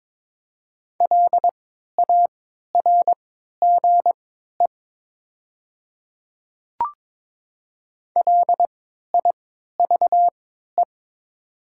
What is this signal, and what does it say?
Morse code, heard as a keyed tone, about 22 wpm, spaced as for 12 wpm, 700 Hz. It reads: LARGE LIVE
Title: 3x Character Spacing - Top 300 Words - 22wpm